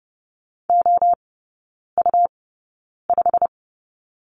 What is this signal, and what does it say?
Morse code, heard as a keyed tone, about 30 wpm, spaced as for 9 wpm, 700 Hz. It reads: OU5
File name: Lesson 17e - Letter u - Three Characters 30wpm